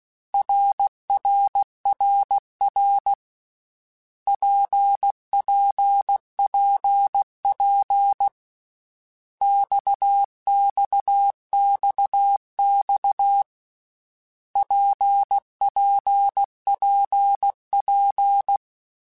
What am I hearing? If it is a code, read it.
Morse code, heard as a keyed tone, about 16 wpm, 800 Hz. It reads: RRRR PPPP XXXX PPPP
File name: Basic Letter Groups By7Groups / RKPX / RKPX@13